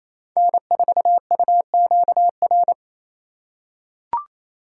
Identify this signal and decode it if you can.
Morse code, heard as a keyed tone, about 28 wpm, 700 Hz. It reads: N4UQR
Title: USA Call Signs - 5 Characters - 4x Character Spacing - 28wpm